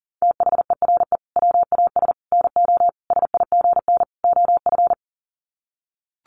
Morse code, heard as a keyed tone, about 40 wpm, 700 Hz. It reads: THERE WAS NO SIGN OF
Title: Sets of 5 Words Encore2 40wpm